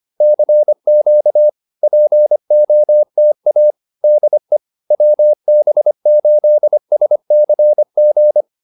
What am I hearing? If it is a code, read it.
Morse code, heard as a keyed tone, about 25 wpm, 590 Hz. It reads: CQ POTA DE WB8SCG